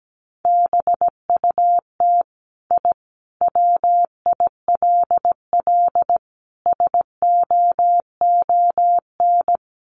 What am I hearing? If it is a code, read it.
Morse code, heard as a keyed tone, about 17 wpm, 700 Hz. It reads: BUT I WILL SOON
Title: Sentences from Top 1300 Words 17wpm